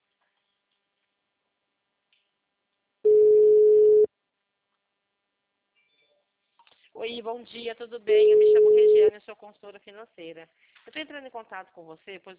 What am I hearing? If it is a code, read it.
Ringback tone (Europe, 425 Hz), with speech.
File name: out- -2001- - - .23021